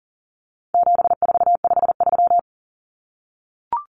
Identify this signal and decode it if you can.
Morse code, heard as a keyed tone, about 40 wpm, 700 Hz. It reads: 7453
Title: Numbers - 4-Digit 40wpm